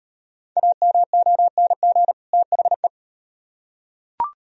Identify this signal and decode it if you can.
Morse code, heard as a keyed tone, about 38 wpm, 700 Hz. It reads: AMONG THE